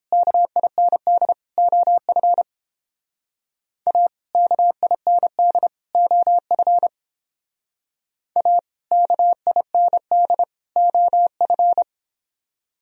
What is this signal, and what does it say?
Morse code, heard as a keyed tone, about 29 wpm, 700 Hz. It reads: KIND OF A KIND OF A KIND OF